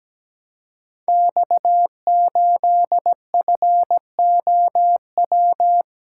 Morse code, heard as a keyed tone, about 17 wpm, 700 Hz. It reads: X8FOW
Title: Code Groups - 5 Letter:Numbers - 17wpm